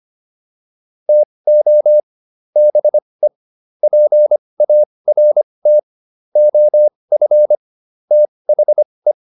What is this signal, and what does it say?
Morse code, heard as a keyed tone, about 25 wpm, spaced as for 20 wpm, 600 Hz. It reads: TO BE PART OF THE
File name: WD405 20